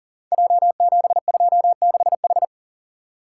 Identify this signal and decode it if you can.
Morse code, heard as a keyed tone, about 39 wpm, 700 Hz. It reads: J726H
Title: Code Groups - 5 Letter:Numbers - 40wpm